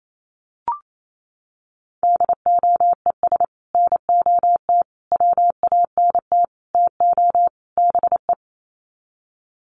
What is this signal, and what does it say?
Morse code, heard as a keyed tone, about 28 wpm, 700 Hz. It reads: DOES NOT WANT TO BE
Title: Sets of 5 Words Encore 28wpm